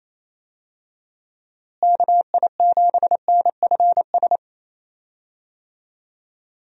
Morse code, heard as a keyed tone, about 27 wpm, 700 Hz. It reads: KI7NFS
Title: USA Call Signs - 6 Characters - Repeated 2x - 28wpm